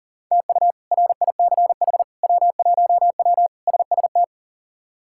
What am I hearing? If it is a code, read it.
Morse code, heard as a keyed tone, about 40 wpm, 700 Hz. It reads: TU RICH W1W SST
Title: K1USN SST QSO Practice - 40wpm